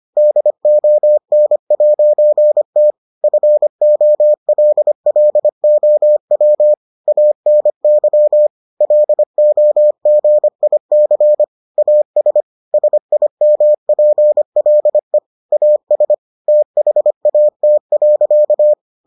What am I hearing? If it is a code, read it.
Morse code, heard as a keyed tone, about 25 wpm, 600 Hz. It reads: DON'T FOLLOW ANY LOGIC AS SIMPLE AS THAT.